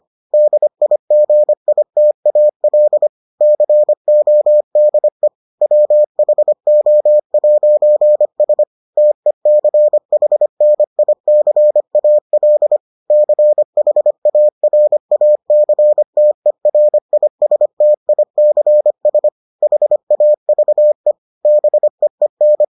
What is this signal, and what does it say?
Morse code, heard as a keyed tone, about 25 wpm, 600 Hz. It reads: DIGITAL CODE WHO'S TECHNICAL CHARACTERISTICS HAVE BEEN